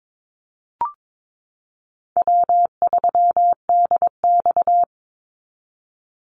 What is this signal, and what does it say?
Morse code, heard as a keyed tone, about 22 wpm, 700 Hz. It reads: W3DX
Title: NA SOTA Activator Callsigns 22wpm